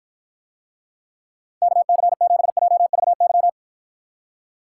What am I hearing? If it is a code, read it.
Morse code, heard as a keyed tone, about 53 wpm, 700 Hz. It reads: KC7JVY